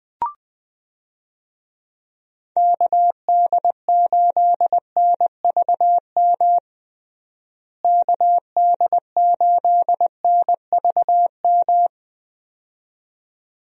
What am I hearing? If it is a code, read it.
Morse code, heard as a keyed tone, about 21 wpm, 700 Hz. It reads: KD8NVM KD8NVM
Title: USA Call Signs - 6 Characters - Repeated 2x - 20wpm